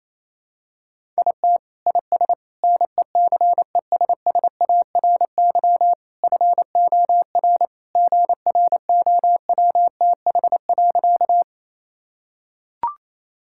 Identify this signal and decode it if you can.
Morse code, heard as a keyed tone, about 27 wpm, 700 Hz. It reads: IT IS NECESSARY FOR GROWTH.
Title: Sentences from Top 1000 Words - Review Part 5 - 28wpm